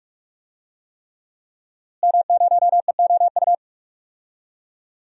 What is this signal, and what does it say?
Morse code, heard as a keyed tone, about 46 wpm, 700 Hz. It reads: M0EOU
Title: DXCC-100 Call Signs - 5 Characters - Repeated 3x - 45wpm